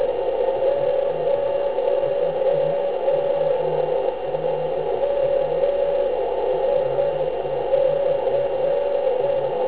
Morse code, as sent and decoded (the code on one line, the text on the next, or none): none